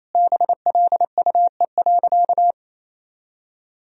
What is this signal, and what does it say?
Morse code, heard as a keyed tone, about 28 wpm, 700 Hz. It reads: BLUE.
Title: Sentences from Top 1000 Words - Review Part 3 - 28wpm